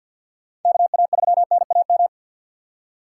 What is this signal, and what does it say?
Morse code, heard as a keyed tone, about 53 wpm, 700 Hz. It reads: KA3NAM
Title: K1USN SST QSO Practice - 50wpm